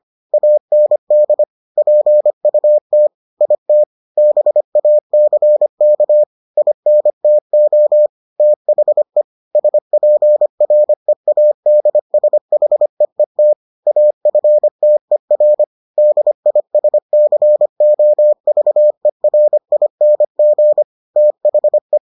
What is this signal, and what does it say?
Morse code, heard as a keyed tone, about 25 wpm, 600 Hz. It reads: AND PUT IT BACK INTO THE SPREADSHEET AFTER DISCOVERING THE